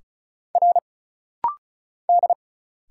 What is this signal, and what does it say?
Morse code, heard as a keyed tone, about 33 wpm, 700 Hz. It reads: R D
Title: Single Letter-Number Alt - Rapid-Fire 35wpm